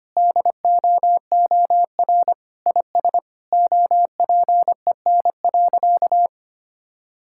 Morse code, heard as a keyed tone, about 25 wpm, 700 Hz. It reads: DOOR IS OPEN.
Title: Sentences from Top 1600 Words 25wpm